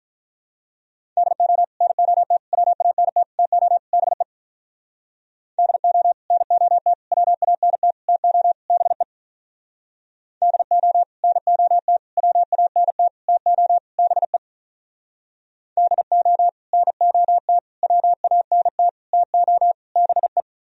Morse code, heard as a keyed tone, about 46 wpm, 700 Hz. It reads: DO NOT WANT TO BE DO NOT WANT TO BE DO NOT WANT TO BE DO NOT WANT TO BE